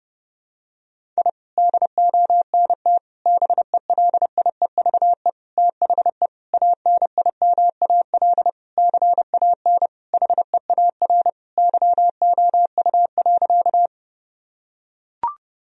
Morse code, heard as a keyed tone, about 30 wpm, 700 Hz. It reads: I DONT BELIEVE THE ANIMAL CAN HEAR YOU.
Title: Sentences from Top 1000 Words - Review Part 1 - 30wpm